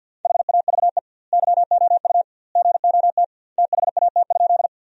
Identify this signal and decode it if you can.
Morse code, heard as a keyed tone, about 51 wpm, 700 Hz. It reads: HAVE YOU GOT THAT?
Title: 2x Word-Spacing - Sentences from Top 300 Words - 50wpm